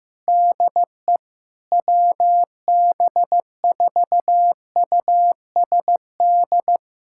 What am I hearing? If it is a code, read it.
Morse code, heard as a keyed tone, about 15 wpm, 700 Hz. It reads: DE WB4USD
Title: USA Callsign DE Callsign - 6 Characters - 15wpm